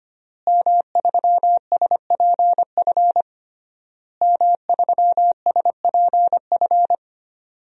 Morse code, heard as a keyed tone, about 25 wpm, 700 Hz. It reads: M3SPF M3SPF
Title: DXCC-100 Call Signs - 5 Characters - Repeated 3x - 25wpm